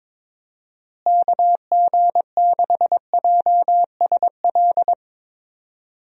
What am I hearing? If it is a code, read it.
Morse code, heard as a keyed tone, about 22 wpm, 700 Hz. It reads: KG6JSL
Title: USA Call Signs - 6 Characters - Repeated 3x - 22wpm